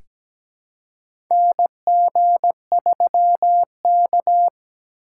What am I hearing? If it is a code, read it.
Morse code, heard as a keyed tone, about 17 wpm, 700 Hz. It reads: NG3K
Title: USA Call Signs - 4x Character Spacing - 17wpm